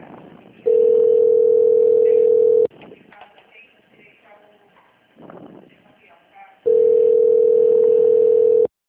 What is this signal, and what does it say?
Ringback tone (North America), 440 + 480 Hz, with speech.
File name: out- -2021- - - .231